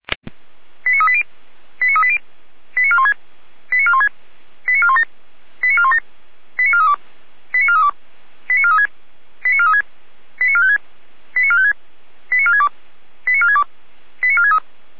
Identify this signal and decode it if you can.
Five-tone selective call (ZVEI), heard as quick runs of five tones, tones of about 70 ms. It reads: 89380 89380 89416 89416 89417 89417 89432 89432 89436 89436 89456 89456 89462 89462 89463